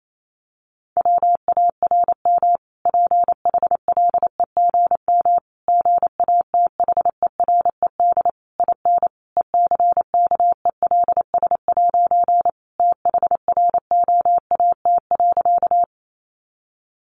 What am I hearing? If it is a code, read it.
Morse code, heard as a keyed tone, about 28 wpm, 700 Hz. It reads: WARM PHLEGM GATHERED IN ECKELS' THROAT.
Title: A Sound Of Thunder 28wpm